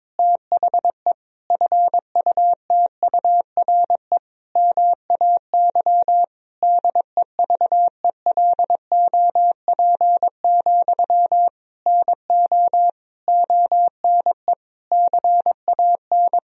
Morse code, heard as a keyed tone, about 22 wpm, 700 Hz. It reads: THE FUTURE MAY DEVELOP, NO ONE CAN
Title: Sentences from Top 1000 Words - Review Part 2 - 22wpm